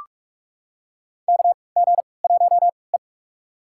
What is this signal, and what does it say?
Morse code, heard as a keyed tone, about 46 wpm, spaced as for 25 wpm, 700 Hz. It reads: KG1E
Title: USA Call Signs - 3x Character Spacing - 45wpm